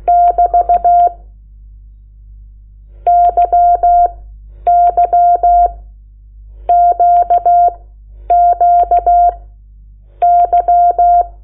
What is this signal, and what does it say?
Morse code, heard as a keyed tone, about 16 wpm, spaced as for 8 wpm, 700 Hz. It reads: <BT> YYQQY
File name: Lecciones / CLASE Nº8